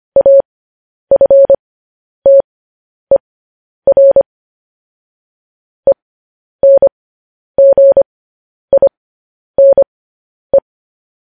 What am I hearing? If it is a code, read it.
Morse code, heard as a keyed tone, about 25 wpm, spaced as for 10 wpm, 550 Hz. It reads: AFTER ENGINE